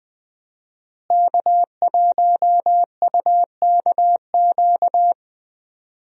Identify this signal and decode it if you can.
Morse code, heard as a keyed tone, about 20 wpm, 700 Hz. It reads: K1UKQ
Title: USA Call Signs - 5 Characters - Repeated 2x - 20wpm